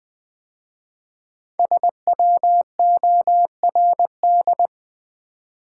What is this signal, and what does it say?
Morse code, heard as a keyed tone, about 20 wpm, 700 Hz. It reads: SWORD
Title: 5 Letter Words - First 3 Letters Given - 20wpm